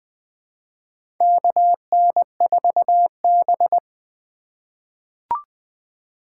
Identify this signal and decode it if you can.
Morse code, heard as a keyed tone, about 20 wpm, 700 Hz. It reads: KN4B
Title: USA Call Signs - 4 Characters - 3x Character Spacing - 20wpm